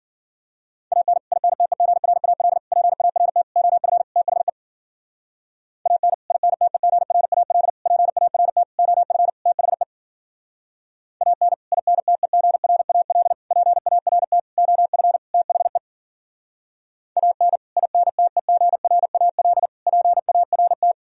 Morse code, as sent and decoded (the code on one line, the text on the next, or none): .- -. / .. -. - . --. .-. .- .-.. / .--. .- .-. - / --- ..-. / - .... / .- -. / .. -. - . --. .-. .- .-.. / .--. .- .-. - / --- ..-. / - .... / .- -. / .. -. - . --. .-. .- .-.. / .--. .- .-. - / --- ..-. / - .... / .- -. / .. -. - . --. .-. .- .-.. / .--. .- .-. -
AN INTEGRAL PART OF THE AN INTEGRAL PART OF THE AN INTEGRAL PART OF THE AN INTEGRAL PART